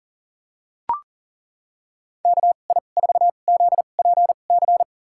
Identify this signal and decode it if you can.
Morse code, heard as a keyed tone, about 40 wpm, spaced as for 29 wpm, 700 Hz. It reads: KI4ZPC
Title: USA Call Signs - 6 Characters - 2x Character Spacing - 40wpm